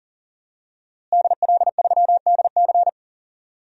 Morse code, heard as a keyed tone, about 38 wpm, 700 Hz. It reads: DL3DC
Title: DXCC-100 Call Signs - 5 Characters - Repeated 3x - 40wpm